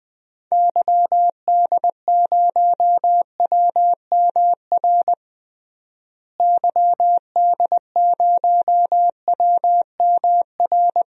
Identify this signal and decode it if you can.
Morse code, heard as a keyed tone, about 21 wpm, 700 Hz. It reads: YD0WMR YD0WMR
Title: DXCC-100 Call Signs - 6 Characters - Repeated 3x - 20wpm